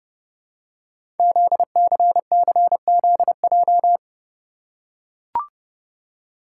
Morse code, heard as a keyed tone, about 30 wpm, 700 Hz. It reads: ZCCZJ
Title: Code Groups - 5 Letters - 30wpm